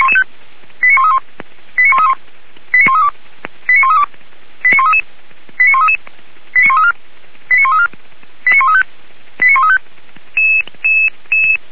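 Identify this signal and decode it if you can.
Five-tone selective call (ZVEI), heard as quick runs of five tones, tones of about 70 ms. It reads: 89131 89131 89132 89132 89133 89133 89134 89134 89135 89135
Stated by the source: Five-tone ZVEI selective call, received on an SDR.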